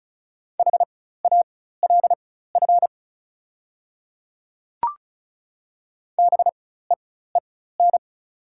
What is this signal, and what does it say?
Morse code, heard as a keyed tone, about 34 wpm, spaced as for 16 wpm, 700 Hz. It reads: HALF BEEN
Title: 4x Character Spacing - Top 500 Words - 35wpm